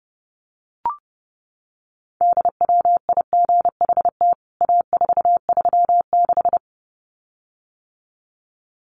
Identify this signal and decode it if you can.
Morse code, heard as a keyed tone, about 30 wpm, 700 Hz. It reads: DWIGHT A436
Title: CWT Exchange 30wpm